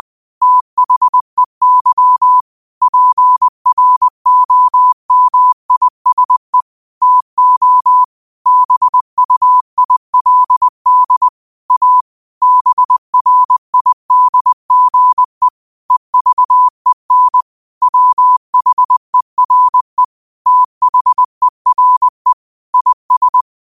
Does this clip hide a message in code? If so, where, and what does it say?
Yes: Morse code from 0.4 to 23.4 s, heard as a keyed tone, about 20 wpm, 1000 Hz. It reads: THEY PROMISE TO BUILD A BRIDGE EVEN WHERE THERE IS